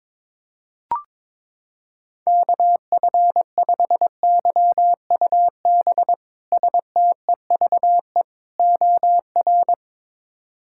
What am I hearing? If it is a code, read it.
Morse code, heard as a keyed tone, about 22 wpm, 700 Hz. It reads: KF5YUB STEVE OR
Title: K1USN SST QSO Practice - 22wpm